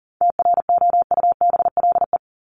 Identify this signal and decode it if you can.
Morse code, heard as a keyed tone, about 40 wpm, 700 Hz. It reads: TROUBLE
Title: Top 1001-1250 Words 40wpm